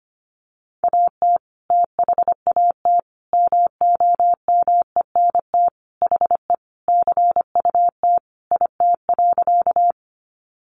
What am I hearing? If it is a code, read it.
Morse code, heard as a keyed tone, about 25 wpm, 700 Hz. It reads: AT THAT MOMENT HE CUT IT.
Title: Sentences from Top 1000 Words 25wpm